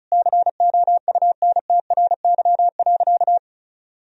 Morse code, heard as a keyed tone, about 34 wpm, 700 Hz. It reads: COUNTRY.
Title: Sentences from Top 300 Words and Abbreviations 35wpm